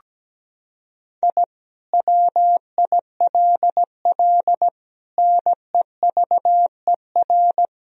Morse code, heard as a keyed tone, about 17 wpm, 700 Hz. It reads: I WILL NEVER
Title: Sentences from Top 1000 Words - Review Part 1 - 17wpm